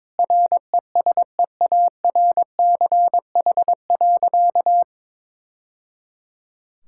Morse code, heard as a keyed tone, about 21 wpm, 700 Hz. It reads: RESEARCH.
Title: Sentences from Top 1000 Words - Review Part 5 - 22wpm